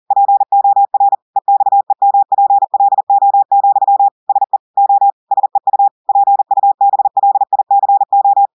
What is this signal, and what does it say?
Morse code, heard as a keyed tone, about 40 wpm, 800 Hz. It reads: POR EXEMPLO, SE O SEU PUBLICO
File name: audios / portugais / portugais texte en clair 2 vitesse 40 mots minutes